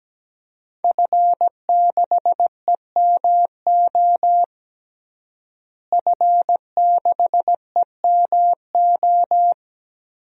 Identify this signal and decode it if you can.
Morse code, heard as a keyed tone, about 17 wpm, 700 Hz. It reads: F6EMO F6EMO
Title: DXCC-100 Call Signs - 5 Characters - Repeated 3x - 17wpm